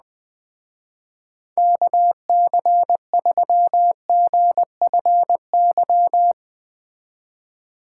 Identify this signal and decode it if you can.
Morse code, heard as a keyed tone, about 19 wpm, 700 Hz. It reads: KC3GFY
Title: Speed Racing - USA 2x3 Call Signs 17wpm